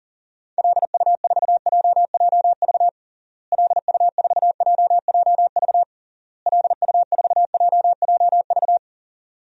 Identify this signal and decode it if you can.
Morse code, heard as a keyed tone, about 40 wpm, 700 Hz. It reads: LU4JJV LU4JJV LU4JJV
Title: DXCC-100 Call Signs - 6 Characters - Repeated 3x - 40wpm